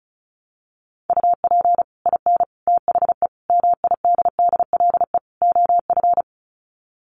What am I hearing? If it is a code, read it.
Morse code, heard as a keyed tone, about 34 wpm, 700 Hz. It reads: UP IN THE MIDDLE OF
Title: Sets of 5 Words Encore 35wpm